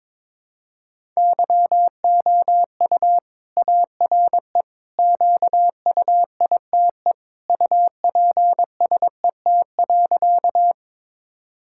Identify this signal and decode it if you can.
Morse code, heard as a keyed tone, about 22 wpm, 700 Hz. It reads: YOU ARE QUITE UPSET.
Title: Sentences from Top 1000 Words - Review Part 5 - 22wpm